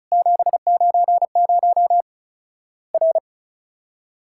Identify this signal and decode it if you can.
Morse code, heard as a keyed tone, about 34 wpm, 700 Hz. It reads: 790 R
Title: Worked All Europe DX Contest - QTC Exchange - 35wpm